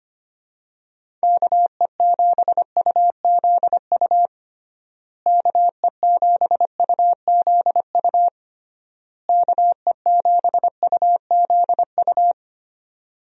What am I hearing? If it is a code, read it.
Morse code, heard as a keyed tone, about 25 wpm, 700 Hz. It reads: KE7UZU KE7UZU KE7UZU
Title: USA Call Signs - 6 Characters - Repeated 3x - 25wpm